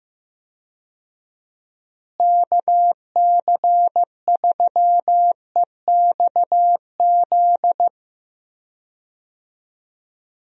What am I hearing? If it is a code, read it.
Morse code, heard as a keyed tone, about 15 wpm, 700 Hz. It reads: KC3EXZ
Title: USA Call Signs - 6 Characters - Repeated 3x - 15wpm